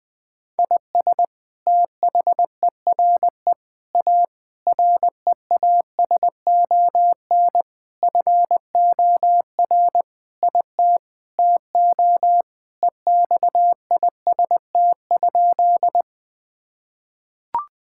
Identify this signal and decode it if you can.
Morse code, heard as a keyed tone, about 20 wpm, 700 Hz. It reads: IS THERE A REASON FOR IT TO EXIST?
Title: Sentences from Top 1000 Words - Review Part 5 - 20wpm